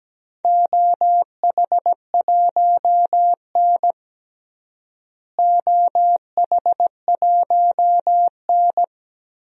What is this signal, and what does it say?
Morse code, heard as a keyed tone, about 17 wpm, 700 Hz. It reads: OH1N OH1N
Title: DXCC-100 Call Signs - 4 Characters - Repeated 3x - 17wpm